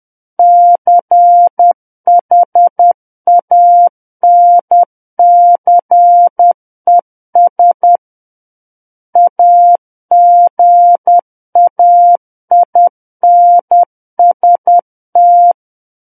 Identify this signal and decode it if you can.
Morse code, heard as a keyed tone, about 10 wpm, 700 Hz. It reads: CHANCES AGAINST
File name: Book One / 01-Chapter 1 - 10 WPM